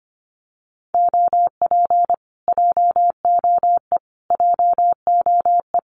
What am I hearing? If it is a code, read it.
Morse code, heard as a keyed tone, about 25 wpm, 700 Hz. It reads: OP JOE JOE
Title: Common QSO Phrases Encore 25wpm